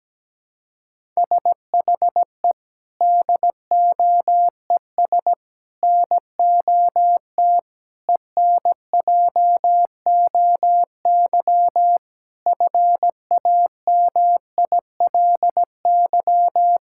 Morse code, heard as a keyed tone, about 17 wpm, 700 Hz. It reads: SHE DOES NOT ENJOY FAMILY
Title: Sentences from Top 1000 Words - Review Part 5 - 17wpm